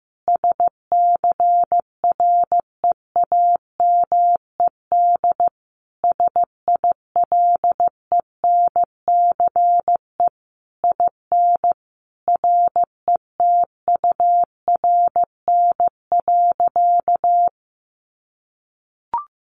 Morse code, heard as a keyed tone, about 15 wpm, 700 Hz. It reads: SCREAMED SILENCE IN RETURN.
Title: A Sound Of Thunder 15wpm